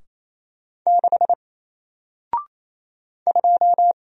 Morse code, heard as a keyed tone, about 28 wpm, 700 Hz. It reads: 6 2
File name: Single-Letter-Number-28wpm